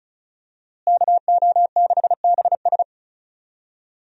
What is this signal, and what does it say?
Morse code, heard as a keyed tone, about 36 wpm, 700 Hz. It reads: KO6BS
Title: USA Call Signs - 5 Characters - Repeated 3x - 35wpm